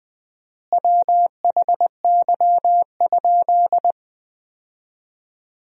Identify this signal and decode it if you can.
Morse code, heard as a keyed tone, about 20 wpm, 700 Hz. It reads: WHY?
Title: 4x Word-Spacing - Sentences from Top 200 Words - 20wpm